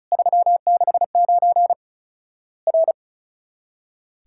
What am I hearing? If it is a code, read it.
Morse code, heard as a keyed tone, about 36 wpm, 700 Hz. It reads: 369 R
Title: Worked All Europe DX Contest - QTC Exchange - 35wpm